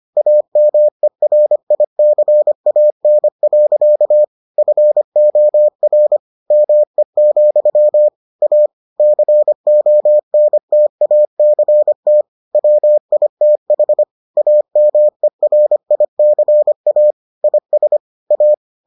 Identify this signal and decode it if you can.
Morse code, heard as a keyed tone, about 25 wpm, 600 Hz. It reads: AMERICAN. FOR ME, A CONTACT WITH AMERICA IS A